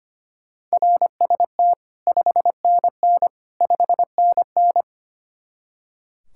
Morse code, heard as a keyed tone, about 30 wpm, 700 Hz. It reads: RST 5NN 5NN
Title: Speed-Racing Common QSO Phrases - 25wpm